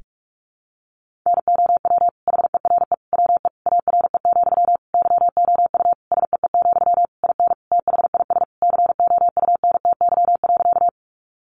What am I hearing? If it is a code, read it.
Morse code, heard as a keyed tone, about 44 wpm, 700 Hz. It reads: NOW HERE WE ARE, YOU SEE, IN THIS COUNTY.
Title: Sentences from Top 1200 Words 45wpm